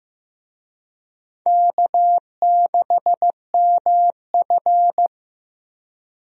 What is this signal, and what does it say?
Morse code, heard as a keyed tone, about 16 wpm, 700 Hz. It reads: K6MF
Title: USA Call Signs - 4 Characters - Repeated 3x - 15wpm